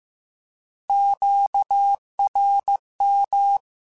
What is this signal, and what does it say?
Morse code, heard as a keyed tone, about 15 wpm, 800 Hz. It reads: QRM
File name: Misc / Misc-mp3s / Words @13